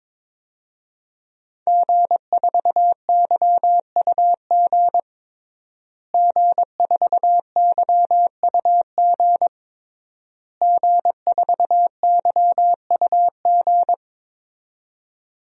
Morse code, heard as a keyed tone, about 22 wpm, 700 Hz. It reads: G4YUG G4YUG G4YUG
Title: DXCC-100 Call Signs - 5 Characters - Repeated 3x - 22wpm